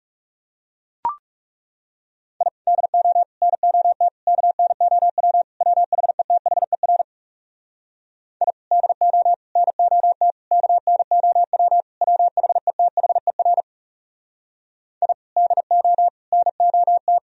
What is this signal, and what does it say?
Morse code, heard as a keyed tone, about 37 wpm, 700 Hz. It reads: I DO NOT KNOW WHETHER I DO NOT KNOW WHETHER I DO NOT
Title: Speed-Racing - Sets of 5 Words 30wpm